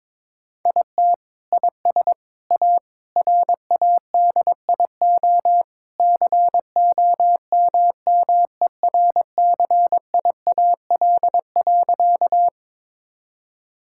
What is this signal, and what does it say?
Morse code, heard as a keyed tone, about 22 wpm, 700 Hz. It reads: IT IS A RADIO COMMERCIAL.
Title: Sentences from Top 1000 Words - Review Part 5 - 22wpm